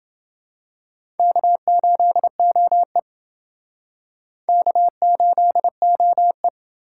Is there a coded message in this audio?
Yes: Morse code from 1.2 to 6.5 s, heard as a keyed tone, about 28 wpm, 700 Hz. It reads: K8OE K8OE